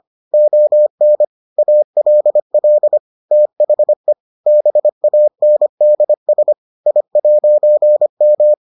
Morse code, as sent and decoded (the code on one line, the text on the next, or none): --- -. / .- .-.. .-.. / - .... / -... .- -. -.. ... / .. .----. --
ON ALL THE BANDS I'M